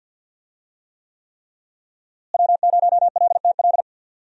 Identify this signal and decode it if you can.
Morse code, heard as a keyed tone, about 47 wpm, 700 Hz. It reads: W0LTL
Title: USA Call Signs - 5 Characters - Repeated 3x - 50wpm